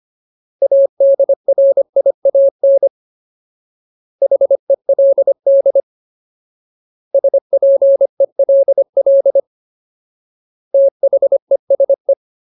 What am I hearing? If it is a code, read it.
Morse code, heard as a keyed tone, about 25 wpm, 550 Hz. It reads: ADRIAN HELD SPELL THESE
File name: Ultimate Word List / 25WPM Ultimate